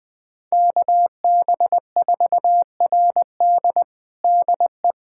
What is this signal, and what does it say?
Morse code, heard as a keyed tone, about 20 wpm, 700 Hz. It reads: KB4RD DE
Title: USA Callsign DE Callsign 20wpm